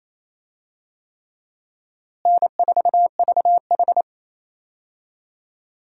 Morse code, heard as a keyed tone, about 28 wpm, 700 Hz. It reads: N4VH
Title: USA Call Signs - 4 Characters - Repeated 3x - 28wpm